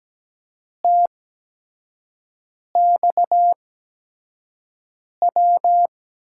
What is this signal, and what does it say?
Morse code, heard as a keyed tone, about 17 wpm, spaced as for 5 wpm, 700 Hz. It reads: TXW
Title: USA Call Signs - 6 Characters - 8x Character Spacing - 17wpm